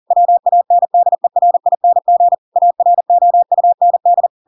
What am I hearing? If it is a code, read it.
Morse code, heard as a keyed tone, about 40 wpm, 700 Hz. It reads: WANDERING AROUND
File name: Pinocchio / Pinocchio ch4 40 40 ik1hge